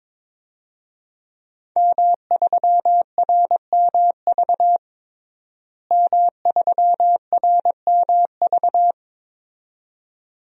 Morse code, heard as a keyed tone, about 22 wpm, 700 Hz. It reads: M3RMV M3RMV
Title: DXCC-100 Call Signs - 5 Characters - Repeated 2x - 22wpm